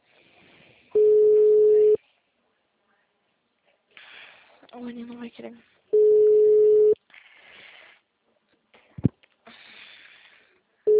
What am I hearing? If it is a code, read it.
Ringback tone (Europe, 425 Hz), with speech.